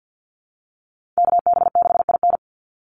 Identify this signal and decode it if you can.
Morse code, heard as a keyed tone, about 50 wpm, 700 Hz. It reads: KB6IN
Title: USA Call Signs 50wpm